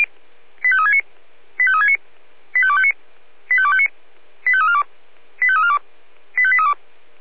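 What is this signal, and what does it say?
Five-tone selective call (ZVEI), heard as quick runs of five tones, tones of about 70 ms. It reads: 85369 85369 85259 85259 85342 85342 85832
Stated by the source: ZVEI five-tone selective call received by SDR.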